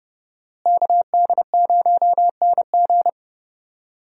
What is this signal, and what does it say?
Morse code, heard as a keyed tone, about 30 wpm, 700 Hz. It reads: KD0NG
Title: USA Call Signs - 5 Characters - Repeated 3x - 30wpm